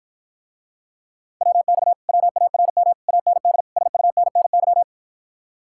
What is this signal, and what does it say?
Morse code, heard as a keyed tone, about 54 wpm, 700 Hz. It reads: WX WARM AND SUNNY